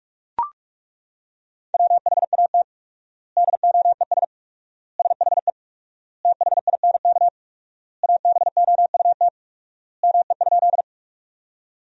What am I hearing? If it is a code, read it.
Morse code, heard as a keyed tone, about 42 wpm, 700 Hz. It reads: WHAT DOES SHE THINK ABOUT ME?
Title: 4x Word-Spacing - Sentences from Top 200 Words - 45wpm